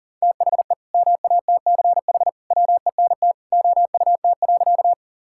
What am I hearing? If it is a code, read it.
Morse code, heard as a keyed tone, about 40 wpm, 700 Hz. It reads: THE MATCH WENT OUT.
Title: Sentences from Top 1600 Words 40wpm